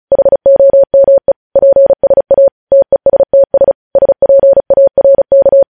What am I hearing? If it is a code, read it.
Morse code, heard as a keyed tone, about 36 wpm, 550 Hz. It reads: HOME PSA TESTS SPARK
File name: podcast / News headlines-1x-35x35